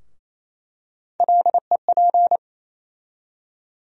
Morse code, heard as a keyed tone, about 28 wpm, 700 Hz. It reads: LEP
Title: USA Call Sign Postfixes - 3 Letters 28wpm